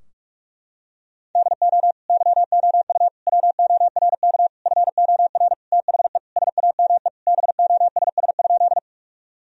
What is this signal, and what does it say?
Morse code, heard as a keyed tone, about 46 wpm, 700 Hz. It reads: DO YOU WORK FOR THE SAME BOSS?
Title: Sentences from Top 1000 Words - Review Part 4 - 45wpm